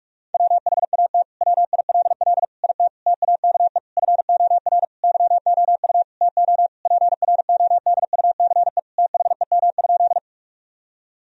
Morse code, heard as a keyed tone, about 44 wpm, 700 Hz. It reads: WHAT WILL IT TAKE FOR YOU TO PRODUCE THEM?
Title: Sentences from Top 1000 Words - Review Part 3 - 45wpm